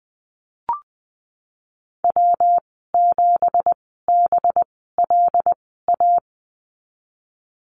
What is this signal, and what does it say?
Morse code, heard as a keyed tone, about 20 wpm, spaced as for 14 wpm, 700 Hz. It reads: W7BLA
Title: 186 Lesson 30g - Letter b - Call Signs 20wpm